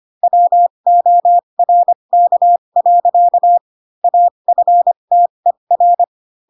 Morse code, heard as a keyed tone, about 25 wpm, spaced as for 22 wpm, 700 Hz. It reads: WORK. AFTER
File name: Pinocchio / Pinocchio ch3 18 25 ik1hge